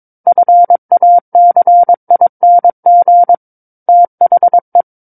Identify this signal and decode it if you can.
Morse code, heard as a keyed tone, about 22 wpm, 700 Hz. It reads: FACING THE
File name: Book One / 13-Chapter 13 - 22 WPM